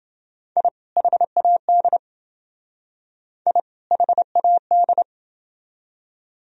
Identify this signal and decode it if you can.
Morse code, heard as a keyed tone, about 28 wpm, 700 Hz. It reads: I HAD I HAD